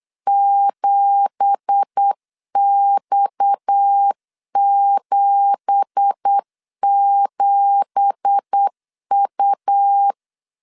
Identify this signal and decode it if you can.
Morse code, heard as a keyed tone, about 8 wpm, 800 Hz. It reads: 7X77U
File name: LEZIONI / 9 2867 8wpm 3440